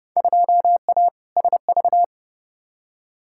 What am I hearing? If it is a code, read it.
Morse code, heard as a keyed tone, about 31 wpm, 700 Hz. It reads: 2A SV
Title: Field Day Exchanges - 30wpm